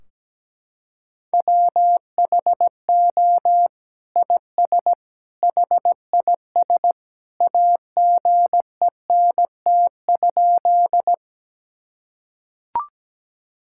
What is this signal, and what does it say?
Morse code, heard as a keyed tone, about 17 wpm, 700 Hz. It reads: WHO IS HIS AGENT?
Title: Sentences from Top 1600 Words 17wpm